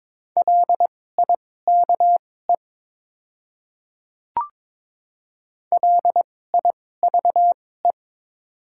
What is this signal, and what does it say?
Morse code, heard as a keyed tone, about 22 wpm, spaced as for 16 wpm, 700 Hz. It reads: LIKE LIVE
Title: 2x Character Spacing - Top 300 Words - 22wpm